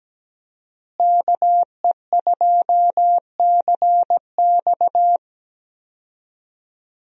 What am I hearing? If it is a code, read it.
Morse code, heard as a keyed tone, about 17 wpm, 700 Hz. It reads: KE2CX
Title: USA Call Signs - 5 Characters - Repeated 2x - 17wpm